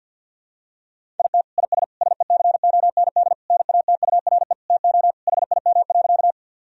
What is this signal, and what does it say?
Morse code, heard as a keyed tone, about 50 wpm, 700 Hz. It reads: IT IS SECOND NATURE TO HIM.
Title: Sentences from Top 1000 Words - Review Part 1 - 50wpm